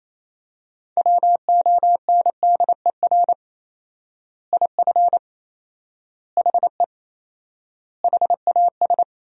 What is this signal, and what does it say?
Morse code, heard as a keyed tone, about 28 wpm, 700 Hz. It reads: WONDER IF HE HAS
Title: 4x Word-Spacing - Sentences from Top 500 Words - 28wpm